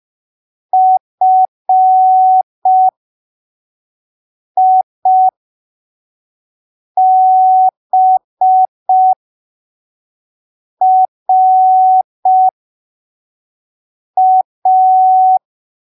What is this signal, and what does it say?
Morse code, heard as a keyed tone, about 5 wpm, spaced as for 3 wpm, 750 Hz. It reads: FIBRA
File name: arquivos / texto / texto 11 05ppm